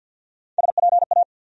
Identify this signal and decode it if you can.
Morse code, heard as a keyed tone, about 53 wpm, 700 Hz. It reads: SPA
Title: USA Call Sign Postfixes - 3 Letters 50wpm